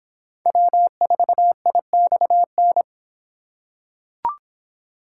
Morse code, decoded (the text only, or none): W4IXN